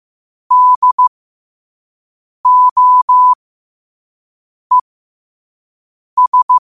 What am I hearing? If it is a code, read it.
Morse code, heard as a keyed tone, about 15 wpm, spaced as for 5 wpm, 1000 Hz. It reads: DOES
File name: qotdfiles / 5wpm